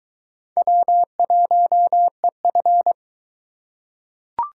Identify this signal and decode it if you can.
Morse code, heard as a keyed tone, about 22 wpm, 700 Hz. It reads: W1EF